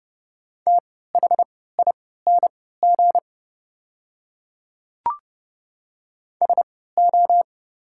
Morse code, heard as a keyed tone, about 30 wpm, spaced as for 17 wpm, 700 Hz. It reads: THING SO
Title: 3x Character Spacing - Top 100 Words - 30wpm